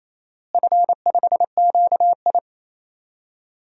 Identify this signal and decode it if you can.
Morse code, heard as a keyed tone, about 28 wpm, 700 Hz. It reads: F5QI
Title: DXCC-100 Call Signs - 4 Characters - Repeated 3x - 28wpm